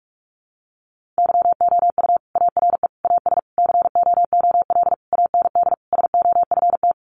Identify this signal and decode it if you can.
Morse code, heard as a keyed tone, about 46 wpm, 700 Hz. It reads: YOU ARE AS COOL AND SOFT
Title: Sentences from Top 1500 Words 45wpm